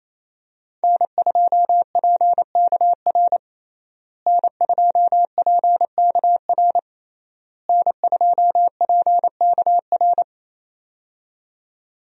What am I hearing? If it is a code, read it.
Morse code, heard as a keyed tone, about 28 wpm, 700 Hz. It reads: N2PKR N2PKR N2PKR
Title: USA Call Signs - 5 Characters - Repeated 3x - 28wpm